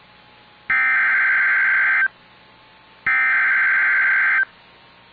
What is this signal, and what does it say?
Emergency Alert System (SAME) tones (header bursts), heard as harsh buzzing bursts. It is heard from 0.7 to 4.4 s.